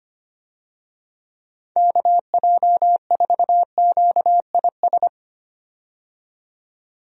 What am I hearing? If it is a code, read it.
Morse code, heard as a keyed tone, about 25 wpm, 700 Hz. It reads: KJ4QIS
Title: USA Call Signs - 6 Characters - 25wpm